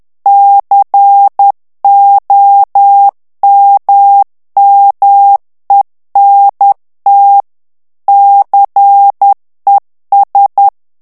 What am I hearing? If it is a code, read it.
Morse code, heard as a keyed tone, about 10 wpm, 800 Hz. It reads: COMMENT CES
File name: morse mp3 / FRANCAIS TEXTE CLAIR 2 VITESSE 10 MOTS MINUTES